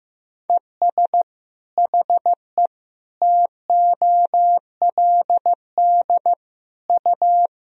Morse code, heard as a keyed tone, about 15 wpm, 700 Hz. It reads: ES HE TOLD U
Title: Sentences from Top 500 Words and Abbreviations 15wpm